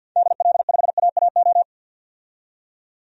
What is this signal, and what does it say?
Morse code, heard as a keyed tone, about 53 wpm, 700 Hz. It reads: DL5AAO
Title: DXCC-100 Call Signs - 6 Characters - 50wpm